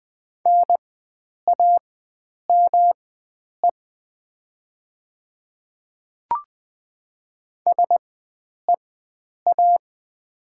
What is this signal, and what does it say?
Morse code, heard as a keyed tone, about 20 wpm, spaced as for 9 wpm, 700 Hz. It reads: NAME SEA
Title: 4x Character Spacing - Top 300 Words - 20wpm